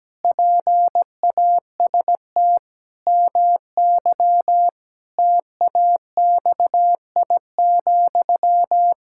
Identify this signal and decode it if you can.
Morse code, heard as a keyed tone, about 17 wpm, 700 Hz. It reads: PAST MY TAXI,
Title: Pangram Challenge v2 - 17wpm